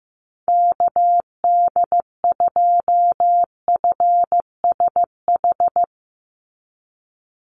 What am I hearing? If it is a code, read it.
Morse code, heard as a keyed tone, about 15 wpm, 700 Hz. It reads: KD2FSH
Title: Call Signs Repeated 2x - 15wpm